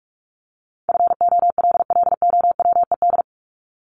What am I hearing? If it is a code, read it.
Morse code, heard as a keyed tone, about 46 wpm, 700 Hz. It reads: FOLLOWED
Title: ED Suffix - Top 100 Words with Base Words 45wpm